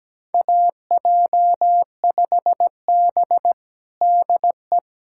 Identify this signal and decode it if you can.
Morse code, heard as a keyed tone, about 17 wpm, 700 Hz. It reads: AJ5B DE
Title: USA Callsign DE Callsign - 4 Characters - 17wpm